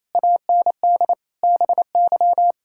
Morse code, heard as a keyed tone, about 28 wpm, 700 Hz. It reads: AND BY
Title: Pangram Challenge v2 - 28wpm